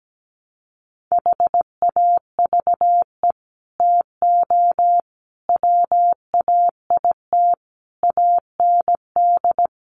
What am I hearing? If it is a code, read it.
Morse code, heard as a keyed tone, about 17 wpm, 700 Hz. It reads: HAVE TO WAIT AND
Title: Sets of 5 Words Encore2 17wpm